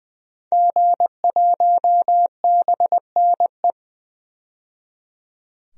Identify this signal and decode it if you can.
Morse code, heard as a keyed tone, about 20 wpm, 700 Hz. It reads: G1BNE
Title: DXCC-100 Call Signs - 5 Characters - 20wpm